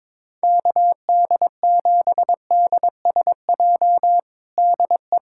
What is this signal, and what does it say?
Morse code, heard as a keyed tone, about 22 wpm, 700 Hz. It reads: KD7DSJ DE
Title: USA Callsign DE Callsign 22wpm